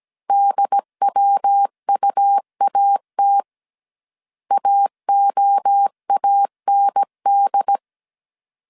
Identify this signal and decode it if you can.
Morse code, heard as a keyed tone, about 17 wpm, 800 Hz. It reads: DWUAT AOAND